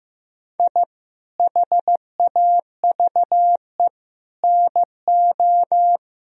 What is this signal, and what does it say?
Morse code, heard as a keyed tone, about 15 wpm, 700 Hz. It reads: I HAVE NO
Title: Sentences from Top 1000 Words - Review Part 3 - 15wpm